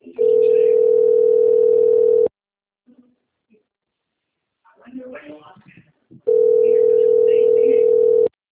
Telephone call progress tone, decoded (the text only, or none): ringback tone